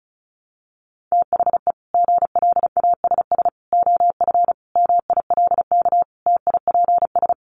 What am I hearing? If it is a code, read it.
Morse code, heard as a keyed tone, about 34 wpm, 700 Hz. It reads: THE GLASS OF MILK TIPS
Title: Additional Easy Phrases 35wpm